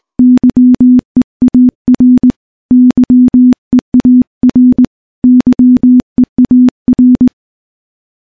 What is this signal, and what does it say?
Morse code, heard as a keyed tone, about 20 wpm, 250 Hz. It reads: YEAR YEAR YEAR
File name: morse-training-20WPM